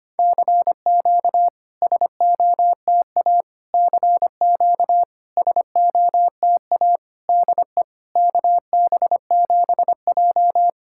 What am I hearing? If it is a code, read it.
Morse code, heard as a keyed tone, about 25 wpm, 690 Hz. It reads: CQ SOTA CQ SOTA DE KB7J
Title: SOTA QSO Practice 25wpm